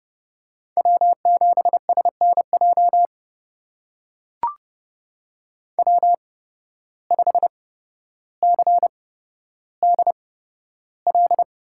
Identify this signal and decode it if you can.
Morse code, heard as a keyed tone, about 30 wpm, spaced as for 13 wpm, 700 Hz. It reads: W7SNJ W5CDL
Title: USA Call Signs - 5 Characters - 8x Character Spacing - 30wpm